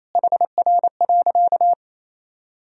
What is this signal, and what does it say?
Morse code, heard as a keyed tone, about 28 wpm, 700 Hz. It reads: HR.
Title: Sentences from Top 300 Words and Abbreviations 28wpm